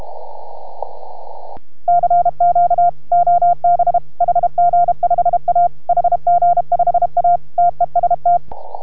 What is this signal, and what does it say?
Morse code, heard as a keyed tone, about 32 wpm, 700 Hz. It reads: CQ OB HG5A HG5A TEST